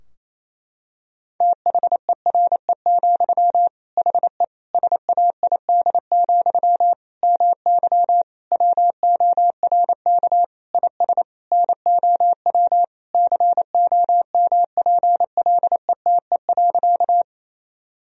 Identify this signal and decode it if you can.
Morse code, heard as a keyed tone, about 28 wpm, 700 Hz. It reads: THERE, HE SAID, MY WORK IS NOW COMPLETE.
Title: Sentences from Top 1000 Words - Review Part 2 - 28wpm